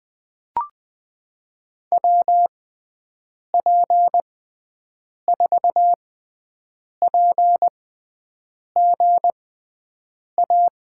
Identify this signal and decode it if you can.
Morse code, heard as a keyed tone, about 20 wpm, spaced as for 7 wpm, 700 Hz. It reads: WP4PGA
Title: USA Call Signs - 6 Characters - 6x Character Spacing - 20wpm